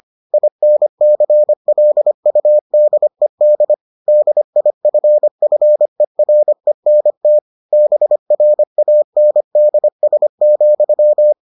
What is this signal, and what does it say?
Morse code, heard as a keyed tone, about 25 wpm, 600 Hz. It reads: INCLUDED DIFFERENT BRANDS,